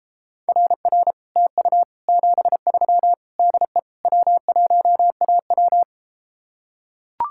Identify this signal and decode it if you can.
Morse code, heard as a keyed tone, about 33 wpm, 700 Hz. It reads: RR TU 73 DE W1AW